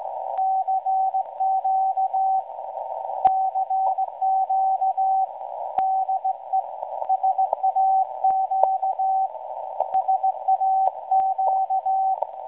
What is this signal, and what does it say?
Morse code, heard as a keyed tone, about 17 wpm, 760 Hz, with radio noise. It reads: CQ CQ DE SAQ SAQ
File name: WebSDR records / SAQ at PI4THT msg 0800-0804 17.2kHz